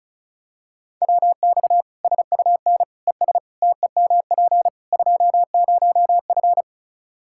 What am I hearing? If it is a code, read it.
Morse code, heard as a keyed tone, about 36 wpm, 700 Hz. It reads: WX SUN ES TEMP 20F
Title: Practice QSO - Exchange 2 - Info - 35wpm